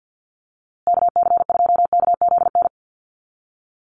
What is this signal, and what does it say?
Morse code, heard as a keyed tone, about 48 wpm, 700 Hz. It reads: KC2KZN
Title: USA Call Signs Repeated 3x 50wpm